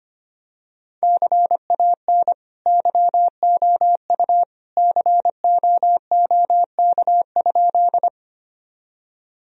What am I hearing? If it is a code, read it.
Morse code, heard as a keyed tone, about 25 wpm, 700 Hz. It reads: CAN YOU COOK?
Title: Sentences from Top 1000 Words - Review Part 4 - 25wpm